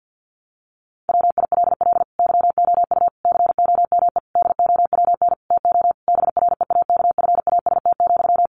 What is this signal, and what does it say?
Morse code, heard as a keyed tone, about 50 wpm, 700 Hz. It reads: WILL YOU COME DOWN TO BREAKFAST,
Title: Sentences from Top 1500 Words 50wpm